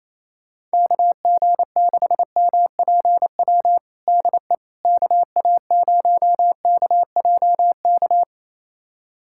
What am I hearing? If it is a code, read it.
Morse code, heard as a keyed tone, about 27 wpm, 700 Hz. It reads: KG6MPW DE KA0KJK
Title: USA Callsign DE Callsign - 6 Characters - 28wpm